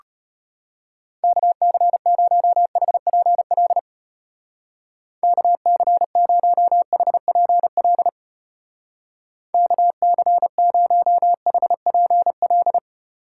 Morse code, heard as a keyed tone, about 33 wpm, 700 Hz. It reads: KC0HPL KC0HPL KC0HPL